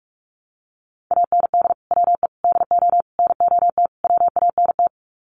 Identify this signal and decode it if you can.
Morse code, heard as a keyed tone, about 44 wpm, 700 Hz. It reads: AND WE DO NOT WANT
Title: Sets of 5 Words Encore2 45wpm